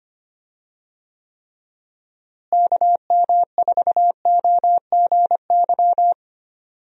Morse code, heard as a keyed tone, about 25 wpm, 700 Hz. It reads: KM4OGY
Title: USA Call Signs - 6 Characters - Repeated 3x - 25wpm